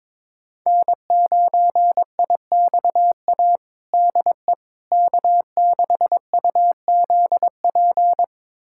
Morse code, heard as a keyed tone, about 22 wpm, 700 Hz. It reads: N9IXA DE K6UZP
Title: USA Callsign DE Callsign - 5 Characters - 22wpm